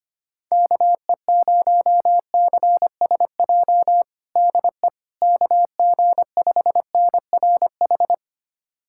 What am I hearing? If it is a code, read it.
Morse code, heard as a keyed tone, about 25 wpm, 700 Hz. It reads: KE0CSJ DE KG5NRH
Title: USA Callsign DE Callsign - 6 Characters - 25wpm